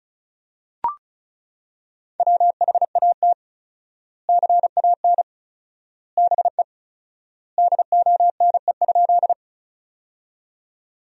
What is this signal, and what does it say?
Morse code, heard as a keyed tone, about 36 wpm, 700 Hz. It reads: WHAT CAN BE DONE?
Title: 4x Word-Spacing - Sentences from Top 500 Words - 35wpm